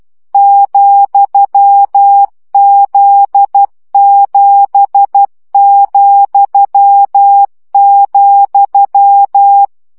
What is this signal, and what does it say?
Morse code, heard as a keyed tone, about 12 wpm, 800 Hz. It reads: ,Z7,,
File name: CW gyakorlas z 7 vesszo 10 baud